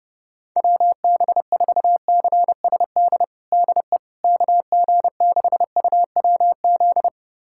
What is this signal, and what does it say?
Morse code, heard as a keyed tone, about 30 wpm, 700 Hz. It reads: WB4CSD DE KG6UWZ
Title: USA Callsign DE Callsign 30wpm